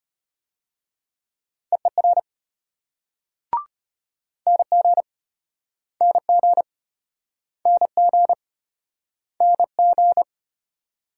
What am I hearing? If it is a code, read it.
Morse code, heard as a keyed tone, about 33 wpm, 700 Hz. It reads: EER NG NG NG NG